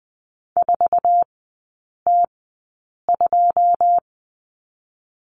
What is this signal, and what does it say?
Morse code, heard as a keyed tone, about 20 wpm, spaced as for 8 wpm, 700 Hz. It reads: 4T2
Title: 081 Lesson 14e - Number 2 - Three Characters 20wpm